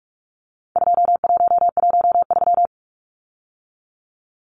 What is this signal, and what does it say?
Morse code, heard as a keyed tone, about 44 wpm, 700 Hz. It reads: 2113
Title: Numbers - 4-Digit 45wpm